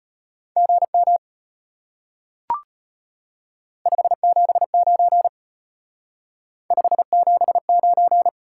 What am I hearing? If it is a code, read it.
Morse code, heard as a keyed tone, about 32 wpm, 700 Hz. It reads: GM 579 579